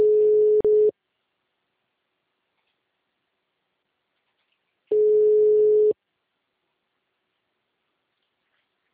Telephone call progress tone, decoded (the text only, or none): ringback tone